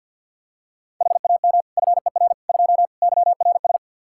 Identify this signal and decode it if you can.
Morse code, heard as a keyed tone, about 46 wpm, 700 Hz. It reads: HAM FER 2 YRS
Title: Practice QSO - Exchange 3 - More Info - 50wpm